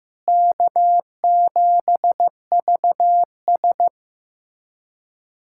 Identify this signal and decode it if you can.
Morse code, heard as a keyed tone, about 15 wpm, 700 Hz. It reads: K7VS
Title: USA Callsign DE Callsign - 4 Characters - 15wpm